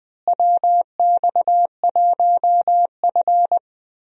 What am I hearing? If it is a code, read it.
Morse code, heard as a keyed tone, about 20 wpm, 700 Hz. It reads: WX1F
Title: USA Call Signs - 4 Characters - Repeated 3x - 20wpm